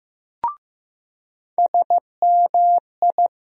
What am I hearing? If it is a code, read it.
Morse code, heard as a keyed tone, about 15 wpm, 700 Hz. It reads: SMI
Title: USA Call Sign Postfixes - 3 Letters 15wpm